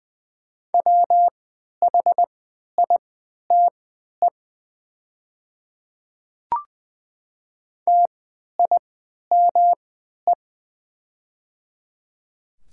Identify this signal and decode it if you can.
Morse code, heard as a keyed tone, about 20 wpm, spaced as for 11 wpm, 690 Hz. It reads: WHITE TIME
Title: 3x Character Spacing - Top 300 Words - 20wpm